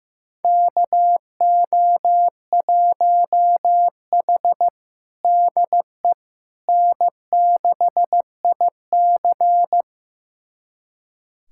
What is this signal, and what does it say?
Morse code, heard as a keyed tone, about 15 wpm, 700 Hz. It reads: KO1H DE N6IC
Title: USA Callsign DE Callsign - 4 Characters - 15wpm